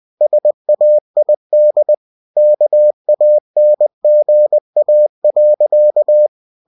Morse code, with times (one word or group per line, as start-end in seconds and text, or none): 0.2-1.9 SAID
2.4-6.3 KANGA.